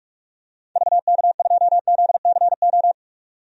Morse code, heard as a keyed tone, about 46 wpm, 700 Hz. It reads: VK2ZCO
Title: DXCC-100 Call Signs - 6 Characters - 45wpm